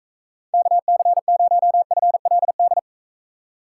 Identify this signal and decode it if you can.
Morse code, heard as a keyed tone, about 42 wpm, 700 Hz. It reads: KC0FLD